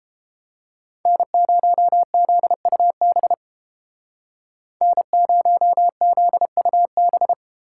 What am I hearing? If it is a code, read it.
Morse code, heard as a keyed tone, about 31 wpm, 700 Hz. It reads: N0ZUB N0ZUB